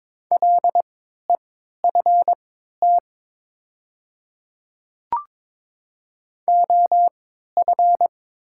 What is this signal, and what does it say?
Morse code, heard as a keyed tone, about 22 wpm, spaced as for 12 wpm, 700 Hz. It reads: LEFT OF
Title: 3x Character Spacing - Top 300 Words - 22wpm